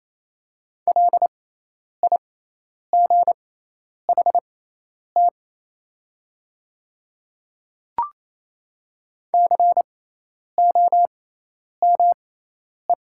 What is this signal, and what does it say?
Morse code, heard as a keyed tone, about 28 wpm, spaced as for 10 wpm, 700 Hz. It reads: LIGHT COME
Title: 6x Character Spacing - Top 200 Words - 28wpm